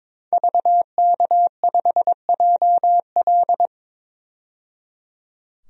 Morse code, heard as a keyed tone, about 21 wpm, 700 Hz. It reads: VK5JL
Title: DXCC-100 Call Signs - 5 Characters - 22wpm